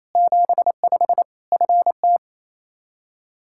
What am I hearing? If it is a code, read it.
Morse code, heard as a keyed tone, about 28 wpm, 700 Hz. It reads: 75 FT
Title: Practice QSO - Exchange 2 - Info - 28wpm